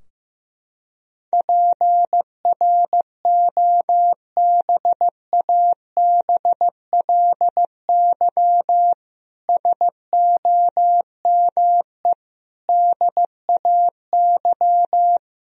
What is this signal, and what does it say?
Morse code, heard as a keyed tone, about 15 wpm, 700 Hz. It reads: PROBABLY SOME DAY
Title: Sentences from Top 1000 Words - Review Part 4 - 15wpm